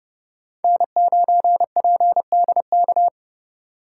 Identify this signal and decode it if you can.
Morse code, heard as a keyed tone, about 32 wpm, 700 Hz. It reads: N9PDK
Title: USA Call Signs - 5 Characters - 3x Character Spacing - 30wpm